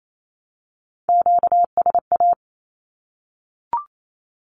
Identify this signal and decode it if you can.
Morse code, heard as a keyed tone, about 28 wpm, 700 Hz. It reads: QSA
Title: Q-Codes 28wpm